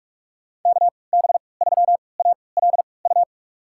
Morse code, heard as a keyed tone, about 47 wpm, spaced as for 26 wpm, 700 Hz. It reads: KB3ALU
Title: USA Call Signs - 6 Characters - 3x Character Spacing - 45wpm